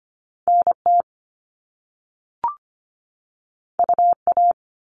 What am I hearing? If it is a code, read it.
Morse code, heard as a keyed tone, about 25 wpm, 700 Hz. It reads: NT UA
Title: Top Two Letter Combinations in Words 25wpm